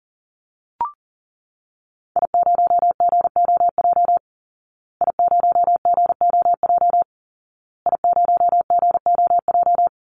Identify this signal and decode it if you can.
Morse code, heard as a keyed tone, about 40 wpm, 700 Hz. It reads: I0GOJ I0GOJ I0GOJ
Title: Call Signs Repeated 2x - 40wpm